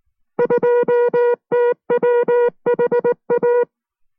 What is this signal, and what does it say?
Morse code, heard as a keyed tone, about 19 wpm, 480 Hz. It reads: 2TWHA